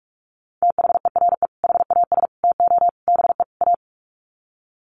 Morse code, heard as a keyed tone, about 44 wpm, 700 Hz. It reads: THERE HAS TO BE A
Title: Sets of 5 Words Encore 45wpm